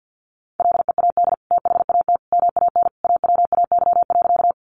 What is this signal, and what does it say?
Morse code, heard as a keyed tone, about 50 wpm, 700 Hz. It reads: LEAD THAT MAN AWAY.
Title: Sentences from Top 600 Words 50wpm